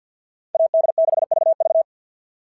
Morse code, heard as a keyed tone, about 52 wpm, 690 Hz. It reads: AD6VV
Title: SOTA QSO Practice 50wpm